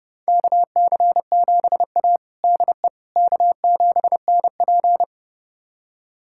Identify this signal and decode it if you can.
Morse code, heard as a keyed tone, about 30 wpm, 700 Hz. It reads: KC7A DE K7NP
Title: USA Callsign DE Callsign - 4 Characters - 30wpm